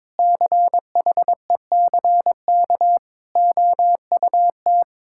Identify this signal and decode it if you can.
Morse code, heard as a keyed tone, about 22 wpm, 700 Hz. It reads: CHECK OUT
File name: practice / 22wpm / Sets of 6 Words - 22wpm